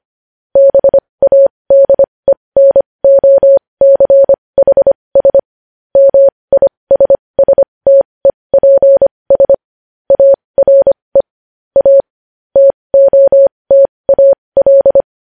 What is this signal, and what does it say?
Morse code, heard as a keyed tone, about 25 wpm, spaced as for 20 wpm, 550 Hz. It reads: BADENOCHS MISSTEPS ARE A TOTAL
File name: podcast / News headlines-1x-25x20